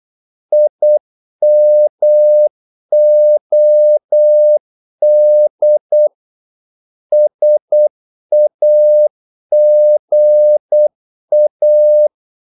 Morse code, heard as a keyed tone, about 8 wpm, 600 Hz. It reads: IMOD SAGA
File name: Korte ord DK 8wpm